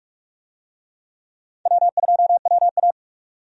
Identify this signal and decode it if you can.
Morse code, heard as a keyed tone, about 45 wpm, 700 Hz. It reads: W2WA
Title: USA Call Signs - 4 Characters - 45wpm